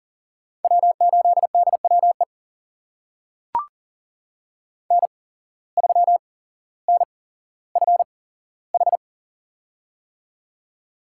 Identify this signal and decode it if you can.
Morse code, heard as a keyed tone, about 40 wpm, spaced as for 17 wpm, 700 Hz. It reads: W8DWE N3NFH
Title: USA Call Signs - 5 Characters - 8x Character Spacing - 40wpm